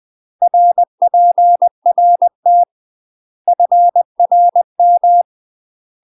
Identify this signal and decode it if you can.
Morse code, heard as a keyed tone, about 20 wpm, 700 Hz. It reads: RPRT FRM
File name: UK MASTER QSOs TRAINING/20WPM